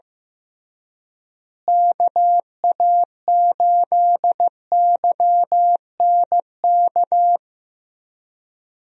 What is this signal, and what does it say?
Morse code, heard as a keyed tone, about 15 wpm, 700 Hz. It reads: KA8YNK
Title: USA Call Signs - 6 Characters - Repeated 3x - 15wpm